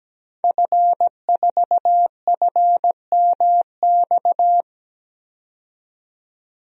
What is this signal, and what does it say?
Morse code, heard as a keyed tone, about 17 wpm, 700 Hz. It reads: F4FMX
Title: DXCC-100 Call Signs - 5 Characters - 17wpm